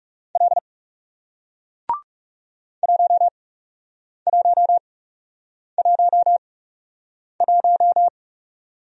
Morse code, heard as a keyed tone, about 40 wpm, spaced as for 8 wpm, 700 Hz. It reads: L 1111